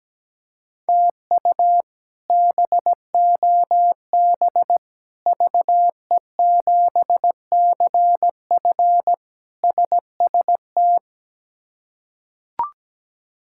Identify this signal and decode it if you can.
Morse code, heard as a keyed tone, about 17 wpm, 700 Hz. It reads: TU BOB VE7CF SST
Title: K1USN SST QSO Practice - 17wpm